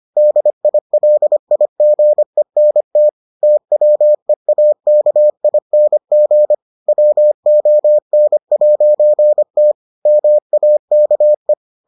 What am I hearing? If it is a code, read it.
Morse code, heard as a keyed tone, about 25 wpm, 600 Hz. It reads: DILIGENT TWEAKING WON'T MAKE